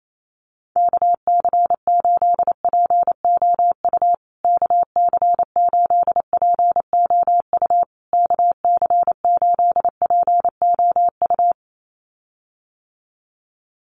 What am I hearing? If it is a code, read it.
Morse code, heard as a keyed tone, about 28 wpm, 700 Hz. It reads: KC8POU KC8POU KC8POU
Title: USA Call Signs Repeated 3x 28wpm